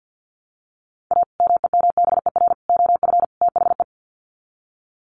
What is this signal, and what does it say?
Morse code, heard as a keyed tone, about 50 wpm, 700 Hz. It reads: A MEMBER OF THE
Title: Sets of 4 Words 50wpm